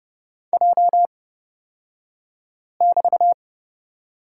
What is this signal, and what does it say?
Morse code, heard as a keyed tone, about 30 wpm, spaced as for 5 wpm, 700 Hz. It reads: J<BT>
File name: Lesson 36d - BT - Two Characters 30wpm